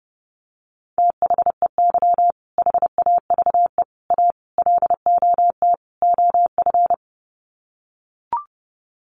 Morse code, heard as a keyed tone, about 30 wpm, 700 Hz. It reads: THEY HAVE A LOT OF
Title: Sets of 5 Words Encore2 30wpm